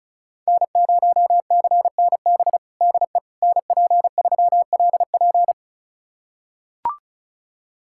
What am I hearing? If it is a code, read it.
Morse code, heard as a keyed tone, about 36 wpm, 690 Hz. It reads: N0CNB DE NP3LP
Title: USA Callsign DE Callsign - 5 Characters - 35wpm